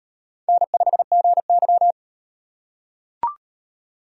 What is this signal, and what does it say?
Morse code, heard as a keyed tone, about 38 wpm, 700 Hz. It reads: N5GY